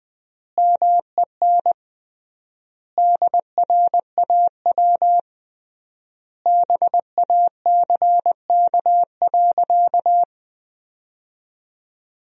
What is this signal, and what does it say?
Morse code, heard as a keyed tone, about 20 wpm, 700 Hz. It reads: MEN DRAW BACK.
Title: 3x Word-Spacing - Sentences from Top 300 Words - 20wpm